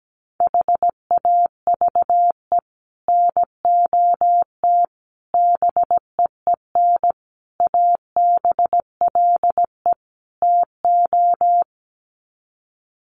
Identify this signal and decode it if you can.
Morse code, heard as a keyed tone, about 17 wpm, 700 Hz. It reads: HAVE NOT BEEN ABLE TO
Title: Sets of 5 Words 17wpm